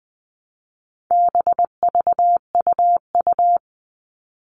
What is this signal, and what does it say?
Morse code, heard as a keyed tone, about 20 wpm, 700 Hz. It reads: BVUU
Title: All Cut Numbers - 4-Digit 20wpm